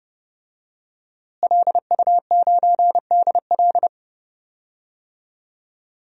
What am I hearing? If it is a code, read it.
Morse code, heard as a keyed tone, about 30 wpm, 700 Hz. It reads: LU9DL
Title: DXCC-100 Call Signs - 5 Characters - Repeated 2x - 30wpm